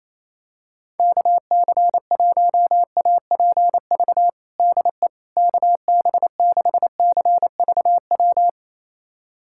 Morse code, heard as a keyed tone, about 28 wpm, 700 Hz. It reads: KC1APV DE KB6CVW
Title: USA Callsign DE Callsign 28wpm